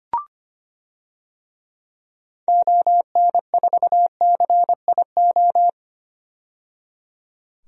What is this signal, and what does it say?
Morse code, heard as a keyed tone, about 25 wpm, 690 Hz. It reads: ON4CIO
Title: DXCC-100 Call Signs - 6 Characters - 25wpm